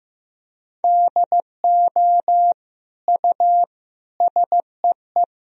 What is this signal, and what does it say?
Morse code, heard as a keyed tone, about 16 wpm, 700 Hz. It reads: DO U SEE
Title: Sentences from Top 100 Words and Abbreviations 15wpm